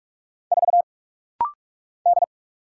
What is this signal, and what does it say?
Morse code, heard as a keyed tone, about 44 wpm, 700 Hz. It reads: 4 D
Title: Single Letter-Number - Rapid-Fire 45wpm